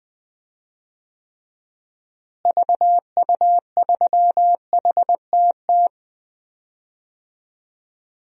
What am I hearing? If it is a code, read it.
Morse code, heard as a keyed tone, about 20 wpm, 700 Hz. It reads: VU3HTT
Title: DXCC-100 Call Signs - 6 Characters - Repeated 2x - 20wpm